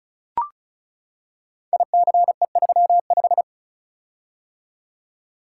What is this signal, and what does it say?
Morse code, heard as a keyed tone, about 36 wpm, 700 Hz. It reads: ICE35
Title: Code Groups - 5 Letter:Numbers - 35wpm